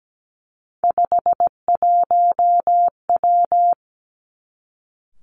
Morse code, heard as a keyed tone, about 17 wpm, 700 Hz. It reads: 51W
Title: Three Challenging Letter-Number Combinations 17wpm